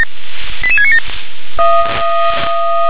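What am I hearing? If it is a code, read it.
Five-tone selective call (ZVEI), heard as quick runs of five tones, tones of about 70 ms. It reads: 88697
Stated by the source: ZVEI five-tone selective call 88693 received by SDR.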